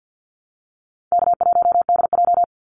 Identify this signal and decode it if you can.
Morse code, heard as a keyed tone, about 48 wpm, 700 Hz. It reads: K1DJ
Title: Call Signs Repeated 3x - 50wpm